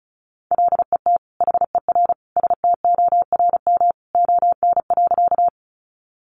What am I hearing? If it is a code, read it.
Morse code, heard as a keyed tone, about 34 wpm, 700 Hz. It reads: LET HER STORM ON.
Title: Sentences from Top 1500 Words 35wpm